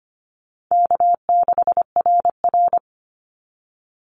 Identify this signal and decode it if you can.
Morse code, heard as a keyed tone, about 25 wpm, 700 Hz. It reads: K6RR
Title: Call Signs Repeated 2x - 25wpm